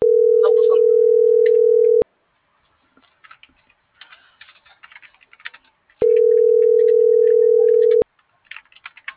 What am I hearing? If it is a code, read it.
Ringback tone (North America), 440 + 480 Hz.